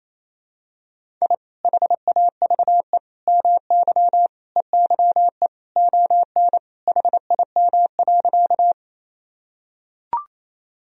Morse code, heard as a keyed tone, about 28 wpm, 700 Hz. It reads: I HAVE MY EYE ON HIM.
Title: Sentences from Top 1000 Words - Review Part 1 - 28wpm